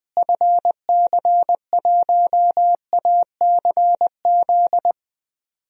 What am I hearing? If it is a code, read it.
Morse code, heard as a keyed tone, about 20 wpm, 700 Hz. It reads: FC1ACZ
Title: DXCC-100 Call Signs - 6 Characters - Repeated 2x - 20wpm